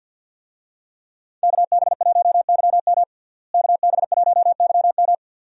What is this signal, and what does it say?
Morse code, heard as a keyed tone, about 52 wpm, 700 Hz. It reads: KB1YM KB1YM
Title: USA Call Signs - 5 Characters - Repeated 2x - 50wpm